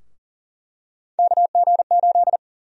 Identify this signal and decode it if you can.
Morse code, heard as a keyed tone, about 40 wpm, 700 Hz. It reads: KG8
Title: USA Call Sign Prefixes - 40wpm